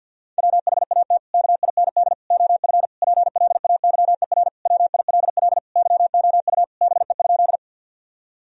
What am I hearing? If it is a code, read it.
Morse code, heard as a keyed tone, about 51 wpm, 700 Hz. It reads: WHAT KIND OF PLAYER WILL YOU BE?
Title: Sentences from Top 1000 Words - Review Part 4 - 50wpm